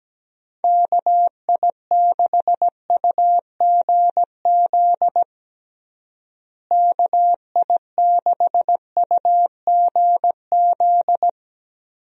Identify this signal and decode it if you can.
Morse code, heard as a keyed tone, about 17 wpm, 700 Hz. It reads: KI6UGZ KI6UGZ
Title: USA Call Signs - 6 Characters - Repeated 2x - 17wpm